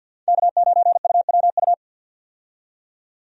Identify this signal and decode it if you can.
Morse code, heard as a keyed tone, about 52 wpm, 700 Hz. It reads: K9UWU
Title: USA Call Signs - 5 Characters - 50wpm